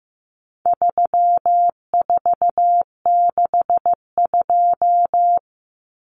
Morse code, heard as a keyed tone, about 15 wpm, 700 Hz. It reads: 3462
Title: Common Cut Numbers - 4-Digit 15wpm